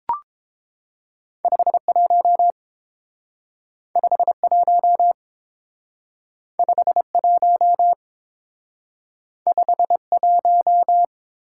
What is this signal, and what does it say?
Morse code, heard as a keyed tone, about 29 wpm, 700 Hz. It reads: 51 51 51 51